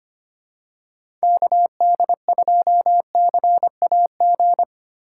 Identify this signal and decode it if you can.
Morse code, heard as a keyed tone, about 25 wpm, 700 Hz. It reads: KD2CAG
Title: USA Call Signs - 4x Character Spacing - 25wpm